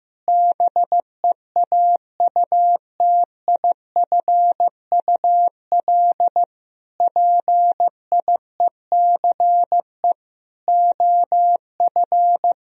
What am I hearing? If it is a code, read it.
Morse code, heard as a keyed tone, about 15 wpm, 700 Hz. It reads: BEAUTIFUL PIECE OF
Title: Sentences from Top 1000 Words - Review Part 2 - 15wpm